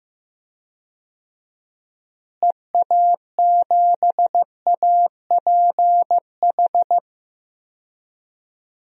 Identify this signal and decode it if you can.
Morse code, heard as a keyed tone, about 15 wpm, 700 Hz. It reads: EA7APH
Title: DXCC-100 Call Signs - 6 Characters - Repeated 3x - 15wpm